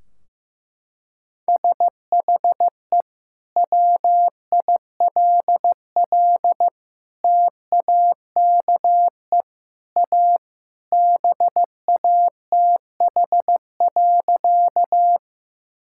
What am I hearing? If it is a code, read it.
Morse code, heard as a keyed tone, about 15 wpm, 700 Hz. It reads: SHE WILL TAKE A BATH.
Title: Sentences from Top 1600 Words 15wpm